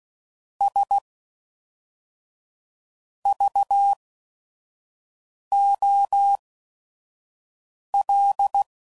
Morse code, heard as a keyed tone, about 16 wpm, 800 Hz. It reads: S V O L